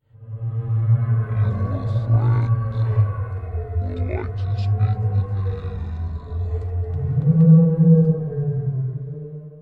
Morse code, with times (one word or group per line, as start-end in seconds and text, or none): none